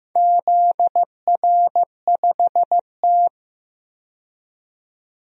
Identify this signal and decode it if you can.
Morse code, heard as a keyed tone, about 15 wpm, 700 Hz. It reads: ZR5T
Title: DXCC-100 Call Signs - 4 Characters - 15wpm